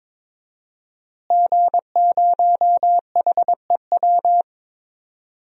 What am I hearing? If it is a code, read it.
Morse code, heard as a keyed tone, about 22 wpm, 700 Hz. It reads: G0HEW
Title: DXCC-100 Call Signs - 5 Characters - Repeated 3x - 22wpm